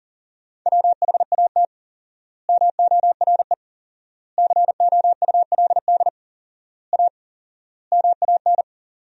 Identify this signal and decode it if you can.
Morse code, heard as a keyed tone, about 37 wpm, 700 Hz. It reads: WHAT MORE COULD A MAN
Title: 4x Word-Spacing - Sentences from Top 200 Words - 40wpm